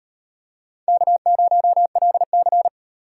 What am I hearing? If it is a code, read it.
Morse code, heard as a keyed tone, about 36 wpm, 700 Hz. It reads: K0LC